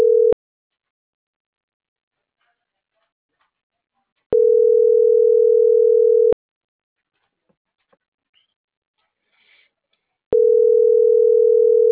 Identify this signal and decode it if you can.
Ringback tone (North America), 440 + 480 Hz.